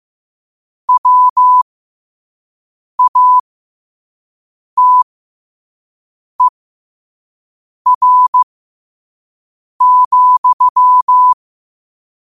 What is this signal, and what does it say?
Morse code, heard as a keyed tone, about 15 wpm, spaced as for 5 wpm, 1000 Hz. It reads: WATER,
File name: qotdfiles / 5wpm